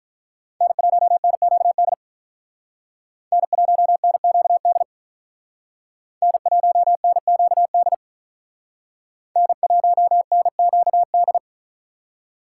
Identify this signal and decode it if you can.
Morse code, heard as a keyed tone, about 45 wpm, 700 Hz. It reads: N1NQD N1NQD N1NQD N1NQD